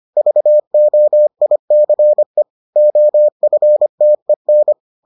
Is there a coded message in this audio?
Yes: Morse code from 0.2 to 4.7 s, heard as a keyed tone, about 25 wpm, 600 Hz. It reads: VOICE OFTEN